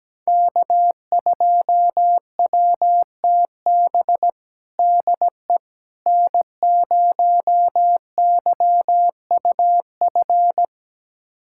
Morse code, heard as a keyed tone, about 17 wpm, 700 Hz. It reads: K2WTB DE N0YUF
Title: USA Callsign DE Callsign - 5 Characters - 17wpm